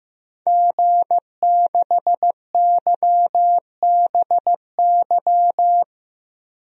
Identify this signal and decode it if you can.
Morse code, heard as a keyed tone, about 15 wpm, 700 Hz. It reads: G6YBY
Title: DXCC-100 Call Signs - 5 Characters - Repeated 2x - 15wpm